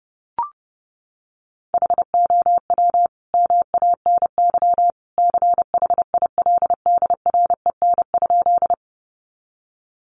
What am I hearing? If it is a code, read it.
Morse code, heard as a keyed tone, about 30 wpm, 700 Hz. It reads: HOW MANY CHILDREN?
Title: Sentences from Top 300 Words 30wpm